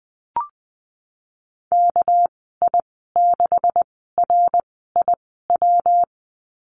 Morse code, heard as a keyed tone, about 20 wpm, spaced as for 15 wpm, 700 Hz. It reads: KI6RIW
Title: 238 Lesson 38g - Letter q - Call Signs 20wpm